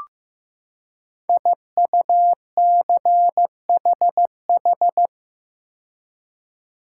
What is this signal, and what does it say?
Morse code, heard as a keyed tone, about 15 wpm, 700 Hz. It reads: IUCHH
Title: Code Groups - 5 Letters - 15wpm